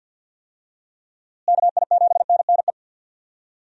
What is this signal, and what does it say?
Morse code, heard as a keyed tone, about 46 wpm, 700 Hz. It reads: KI7NNE
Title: USA Call Signs - 6 Characters - Repeated 2x - 50wpm